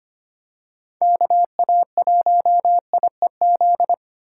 Morse code, heard as a keyed tone, about 25 wpm, 700 Hz. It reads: KA1IEZ
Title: USA Call Signs - 3x Character Spacing - 25wpm